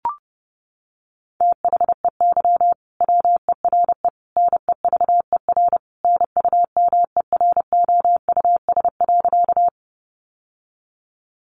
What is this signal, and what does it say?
Morse code, heard as a keyed tone, about 30 wpm, 700 Hz. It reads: THEY WERE NEVER NUMEROUS.
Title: Sentences from Top 1400 Words 30wpm